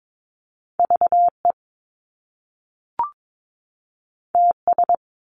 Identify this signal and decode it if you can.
Morse code, heard as a keyed tone, about 22 wpm, 700 Hz. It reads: VE TS
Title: Top Two Letter Combinations in Words 22wpm